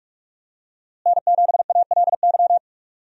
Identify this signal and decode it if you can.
Morse code, heard as a keyed tone, about 46 wpm, 700 Hz. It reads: N7ALY
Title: USA Call Signs - 5 Characters - Repeated 2x - 45wpm